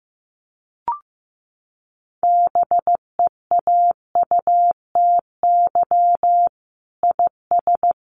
Morse code, heard as a keyed tone, about 15 wpm, 700 Hz. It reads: BEAUTY IS
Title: Sentences from Top 1300 Words 15wpm